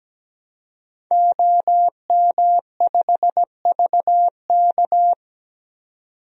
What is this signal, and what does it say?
Morse code, heard as a keyed tone, about 17 wpm, 700 Hz. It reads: OM5VK
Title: DXCC-100 Call Signs - 5 Characters - 17wpm